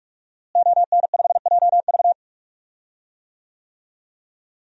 Morse code, heard as a keyed tone, about 43 wpm, 700 Hz. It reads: ON5JV
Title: DXCC-100 Call Signs - 5 Characters - Repeated 2x - 45wpm